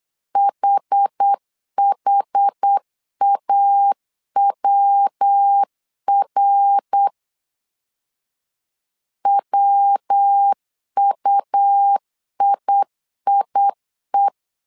Morse code, heard as a keyed tone, about 8 wpm, 800 Hz. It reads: HHAWR WUIIE
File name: CW / DET / 1 3 1500C S B 8wpm 4141